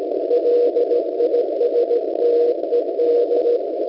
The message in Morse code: LISTED